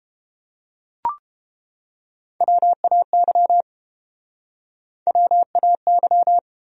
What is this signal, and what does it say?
Morse code, heard as a keyed tone, about 31 wpm, 700 Hz. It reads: WAY WAY